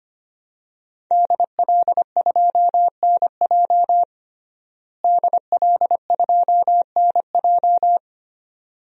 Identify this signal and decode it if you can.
Morse code, heard as a keyed tone, about 25 wpm, 700 Hz. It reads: DL2NJ DL2NJ
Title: DXCC-100 Call Signs - 5 Characters - Repeated 2x - 25wpm